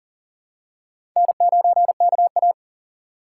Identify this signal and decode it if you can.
Morse code, heard as a keyed tone, about 40 wpm, 700 Hz. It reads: N9KA
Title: USA Call Signs - 4 Characters - 4x Character Spacing - 40wpm